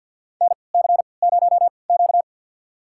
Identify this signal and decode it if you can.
Morse code, heard as a keyed tone, about 52 wpm, spaced as for 28 wpm, 700 Hz. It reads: NC0Q
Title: USA Call Signs - 4 Characters - 3x Character Spacing - 50wpm